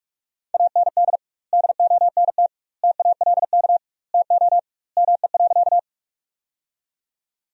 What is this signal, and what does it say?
Morse code, heard as a keyed tone, about 42 wpm, 700 Hz. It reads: AND DONT TALK TO ME.
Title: 2x Word-Spacing - Sentences from Top 400 Words - 45wpm